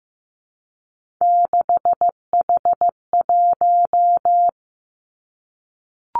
Morse code, heard as a keyed tone, about 15 wpm, 700 Hz. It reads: 6H1
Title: Three Challenging Letter-Number Combinations 15wpm